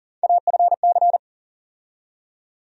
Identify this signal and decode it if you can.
Morse code, heard as a keyed tone, about 40 wpm, 700 Hz. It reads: AFC
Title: USA Call Sign Postfixes - 3 Letters 40wpm